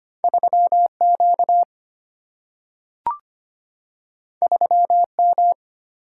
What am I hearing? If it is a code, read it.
Morse code, heard as a keyed tone, about 25 wpm, 700 Hz. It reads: 3 Q 3 M
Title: Single-Letter Single-Number 25wpm